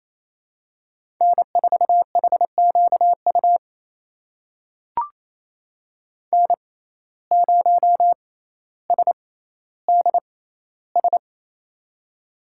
Morse code, heard as a keyed tone, about 28 wpm, spaced as for 14 wpm, 700 Hz. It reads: N4HQU N0SDS
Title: USA Call Signs - 5 Characters - 6x Character Spacing - 28wpm